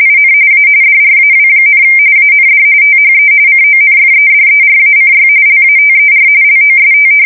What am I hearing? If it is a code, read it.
RTTY, about 45.45 baud. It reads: IN A DRAW, AND THE CONFEDERACY LATER LOST
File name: fsk170 / usnavy3